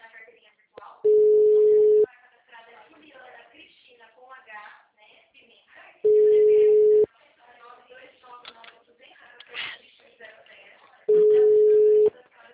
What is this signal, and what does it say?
Ringback tone (Europe, 425 Hz), with speech.